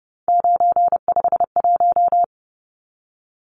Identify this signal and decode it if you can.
Morse code, heard as a keyed tone, about 30 wpm, 700 Hz. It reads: 951
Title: Numbers - 3-Digit 30wpm